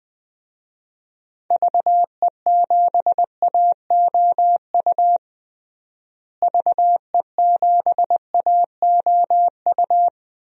Morse code, heard as a keyed tone, about 21 wpm, 700 Hz. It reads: VE7AOU VE7AOU
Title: DXCC-100 Call Signs - 6 Characters - Repeated 2x - 20wpm